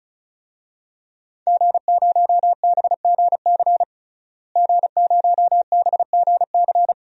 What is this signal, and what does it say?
Morse code, heard as a keyed tone, about 34 wpm, 700 Hz. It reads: G0BGC G0BGC
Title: DXCC-100 Call Signs - 5 Characters - Repeated 2x - 35wpm